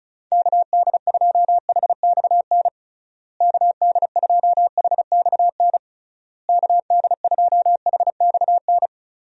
Morse code, heard as a keyed tone, about 34 wpm, 700 Hz. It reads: KD2HXN KD2HXN KD2HXN
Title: USA Call Signs - 6 Characters - Repeated 3x - 35wpm